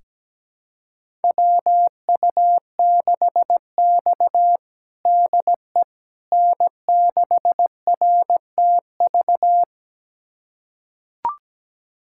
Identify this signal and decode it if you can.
Morse code, heard as a keyed tone, about 17 wpm, 700 Hz. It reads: WU6X DE N6RTV
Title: USA Callsign DE Callsign 17wpm